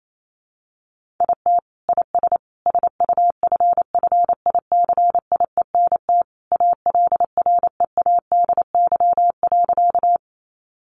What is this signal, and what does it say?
Morse code, heard as a keyed tone, about 28 wpm, 700 Hz. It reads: IT IS SUFFICIENT ALREADY.
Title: Sentences from Top 1400 Words 28wpm